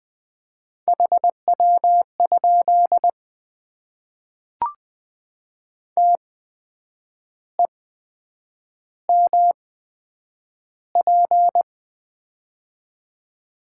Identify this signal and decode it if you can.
Morse code, heard as a keyed tone, about 20 wpm, spaced as for 8 wpm, 700 Hz. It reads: HW? TEMP
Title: Top 90 QSO Elements - 8x Character Spacing - 20wpm